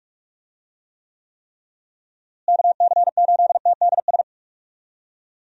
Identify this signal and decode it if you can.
Morse code, heard as a keyed tone, about 41 wpm, 700 Hz. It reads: KC8TDS
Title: USA Call Signs - 6 Characters - Repeated 3x - 45wpm